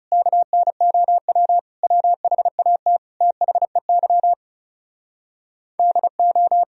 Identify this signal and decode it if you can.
Morse code, heard as a keyed tone, about 37 wpm, 700 Hz. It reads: KNOW WHAT THEY DO
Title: Speed-Racing - Sets of 5 Words 30wpm